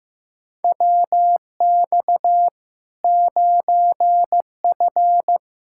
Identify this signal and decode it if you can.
Morse code, heard as a keyed tone, about 15 wpm, 700 Hz. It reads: WX 9F
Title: Practice QSO - Exchange 2 - Info - 15wpm